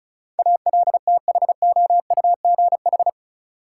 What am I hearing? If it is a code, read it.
Morse code, heard as a keyed tone, about 36 wpm, 700 Hz. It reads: ALTHOUGH
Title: Sentences from Top 600 Words - New Word Drill - 35wpm